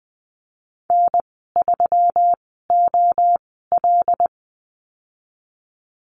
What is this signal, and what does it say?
Morse code, heard as a keyed tone, about 20 wpm, spaced as for 14 wpm, 700 Hz. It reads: N3OL
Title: 122 Lesson 20g - Number 3 - Call Signs 20wpm